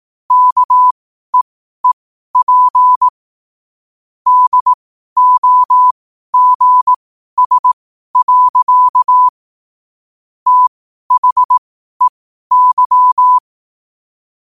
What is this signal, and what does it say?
Morse code, heard as a keyed tone, about 18 wpm, spaced as for 12 wpm, 1000 Hz. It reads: KEEP DOGS. THEY
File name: qotdfiles / 12wpm